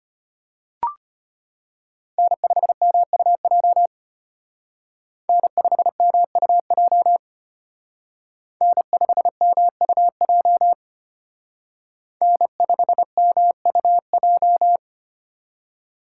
Morse code, heard as a keyed tone, about 29 wpm, 700 Hz. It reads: N5MUJ N5MUJ N5MUJ N5MUJ
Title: Speed Racing - USA 1x3 Call Signs 25wpm